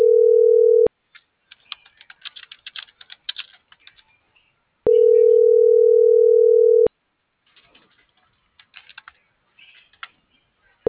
Ringback tone (North America), 440 + 480 Hz.